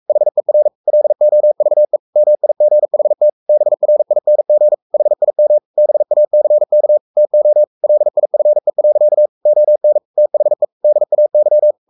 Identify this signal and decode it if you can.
Morse code, heard as a keyed tone, about 44 wpm, 600 Hz. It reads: HER LOVE MIGHT BRING HIM BACK TO LIFE. ON THE DAY
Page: Romanen Never af Ken Follet på engelsk.